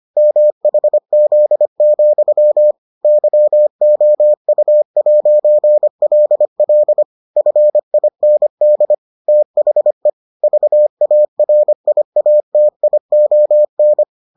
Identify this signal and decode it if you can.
Morse code, heard as a keyed tone, about 25 wpm, 600 Hz. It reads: MHZ, YOU'LL FIND THE VARIATION